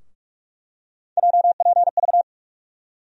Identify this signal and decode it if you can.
Morse code, heard as a keyed tone, about 44 wpm, 700 Hz. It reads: JPV
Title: USA Call Sign Postfixes - 3 Letters 45wpm